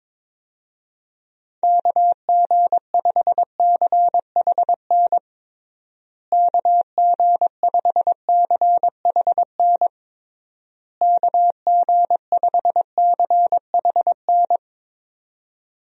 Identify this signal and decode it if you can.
Morse code, heard as a keyed tone, about 22 wpm, 700 Hz. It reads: KG5CHN KG5CHN KG5CHN
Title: USA Call Signs - 6 Characters - Repeated 3x - 22wpm